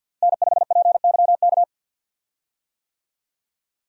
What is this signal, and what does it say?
Morse code, heard as a keyed tone, about 51 wpm, 700 Hz. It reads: N5PYK
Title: USA Call Signs - 5 Characters - Repeated 2x - 50wpm